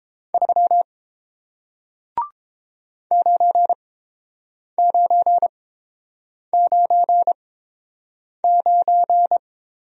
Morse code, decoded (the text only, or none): 3 9999